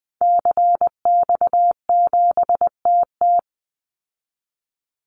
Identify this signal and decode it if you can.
Morse code, heard as a keyed tone, about 20 wpm, 700 Hz. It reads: CX7TT
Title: Call Signs Repeated 3x - 20wpm